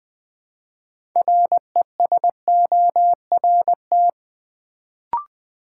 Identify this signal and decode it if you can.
Morse code, heard as a keyed tone, about 20 wpm, 700 Hz. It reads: RESORT
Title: Build A Word - 3 Letters At A Time - 6-Letter Words - 20wpm